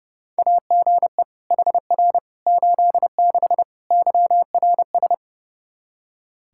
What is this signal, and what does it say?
Morse code, heard as a keyed tone, about 30 wpm, 700 Hz. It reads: AGE HR 86 YRS
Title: Practice QSO - Exchange 3 - More Info - 30wpm